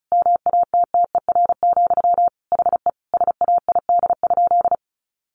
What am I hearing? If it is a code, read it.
Morse code, heard as a keyed tone, about 34 wpm, 700 Hz. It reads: MATTER, HE SAID?
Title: Sentences from Top 700 Words 35wpm